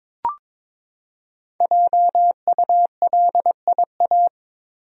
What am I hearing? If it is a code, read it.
Morse code, heard as a keyed tone, about 22 wpm, 700 Hz. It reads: JULIA
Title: USA Common Names - 22wpm